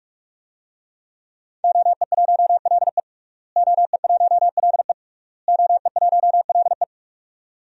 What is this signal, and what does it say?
Morse code, heard as a keyed tone, about 46 wpm, 700 Hz. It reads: OE1LE OE1LE OE1LE
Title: DXCC-100 Call Signs - 5 Characters - Repeated 3x - 45wpm